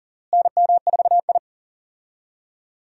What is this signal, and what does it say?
Morse code, heard as a keyed tone, about 40 wpm, 700 Hz. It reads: NM4I
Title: USA Call Signs - 4 Characters - Repeated 3x - 40wpm